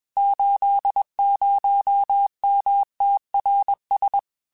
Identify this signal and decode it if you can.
Morse code, heard as a keyed tone, about 21 wpm, 800 Hz. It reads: 80MTRS
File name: QSOs / QSOs-mp3s / WD0FBT 1@20